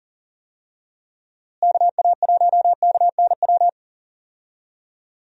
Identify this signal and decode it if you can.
Morse code, heard as a keyed tone, about 40 wpm, 700 Hz. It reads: KA1KNW
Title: USA Call Signs - 6 Characters - Repeated 2x - 40wpm